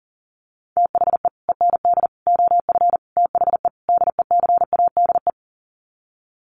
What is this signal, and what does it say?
Morse code, heard as a keyed tone, about 40 wpm, 700 Hz. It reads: THE END OF THE DECADE
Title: Sets of 5 Words Encore 40wpm